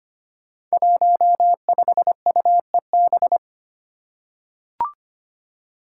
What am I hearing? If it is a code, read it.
Morse code, heard as a keyed tone, about 25 wpm, 700 Hz. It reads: 15UEB
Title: Five Challenging Letter-Number Combinations 25wpm